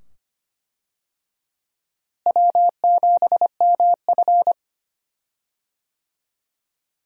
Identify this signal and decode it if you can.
Morse code, heard as a keyed tone, about 25 wpm, 700 Hz. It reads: W7MF
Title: USA Call Signs - 4 Characters - Repeated 2x - 25wpm